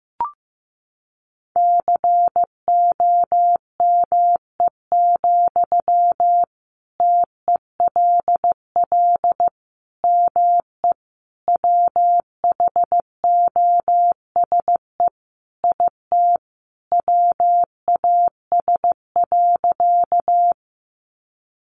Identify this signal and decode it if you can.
Morse code, heard as a keyed tone, about 15 wpm, 700 Hz. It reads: COME, TELL ME WHOSE IT WAS.
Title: Sentences from Top 1200 Words 15wpm